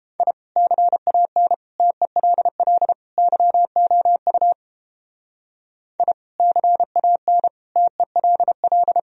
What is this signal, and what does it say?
Morse code, heard as a keyed tone, about 29 wpm, 700 Hz. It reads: I CAN TELL YOU I CAN TELL